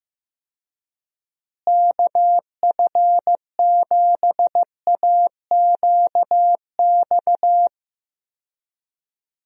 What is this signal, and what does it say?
Morse code, heard as a keyed tone, about 15 wpm, 700 Hz. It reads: KF7AQX
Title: USA Call Signs - 6 Characters - Repeated 3x - 15wpm